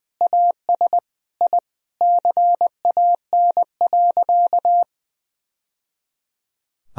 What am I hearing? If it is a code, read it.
Morse code, heard as a keyed tone, about 20 wpm, 700 Hz. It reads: AS I CAN.
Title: Sentences from Top 1000 Words - Review Part 3 - 20wpm